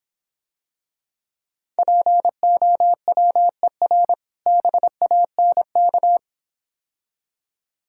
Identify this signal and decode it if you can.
Morse code, heard as a keyed tone, about 26 wpm, 700 Hz. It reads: POWER BANK